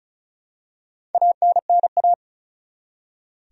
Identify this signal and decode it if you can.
Morse code, heard as a keyed tone, about 34 wpm, 700 Hz. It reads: ANNA
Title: USA Common Names - Max 4 Letters - 35wpm